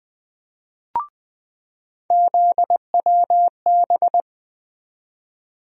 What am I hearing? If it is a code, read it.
Morse code, heard as a keyed tone, about 20 wpm, 700 Hz. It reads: ZWB
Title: USA Call Sign Postfixes - 3 Letters 20wpm